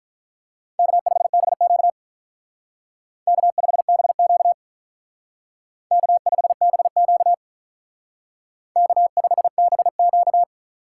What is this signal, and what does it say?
Morse code, heard as a keyed tone, about 45 wpm, 700 Hz. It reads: K5BQ K5BQ K5BQ K5BQ